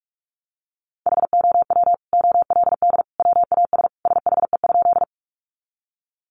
Morse code, heard as a keyed tone, about 44 wpm, 700 Hz. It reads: HOW OLD WAS SHE?
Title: Sentences from Top 200 Words 45wpm